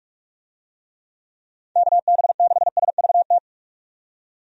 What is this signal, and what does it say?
Morse code, heard as a keyed tone, about 46 wpm, 700 Hz. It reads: KB6SVT
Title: USA Call Signs - 6 Characters - 45wpm